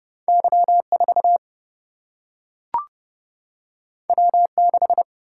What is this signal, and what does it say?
Morse code, heard as a keyed tone, about 30 wpm, 700 Hz. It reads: Y4 W6
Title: Single-Letter Single-Number 30wpm